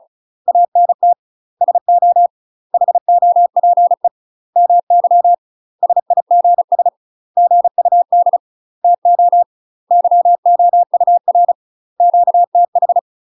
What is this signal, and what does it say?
Morse code, heard as a keyed tone, about 36 wpm, 700 Hz. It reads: ANT SO HOPE MY SIGS GUD TO YOUR QTH
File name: UK MASTER QSOs TRAINING / 35WPM / UK MASTER QSOs 0016